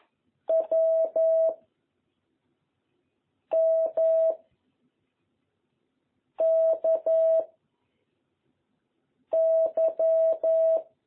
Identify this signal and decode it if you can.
Morse code, heard as a keyed tone, about 11 wpm, 660 Hz. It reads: W M K Y
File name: jh8cbh / 2021cw